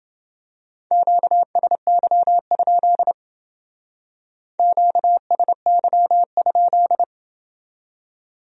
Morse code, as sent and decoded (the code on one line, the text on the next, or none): --.- ... -.-- ..--.. / --.- ... -.-- ..--..
QSY? QSY?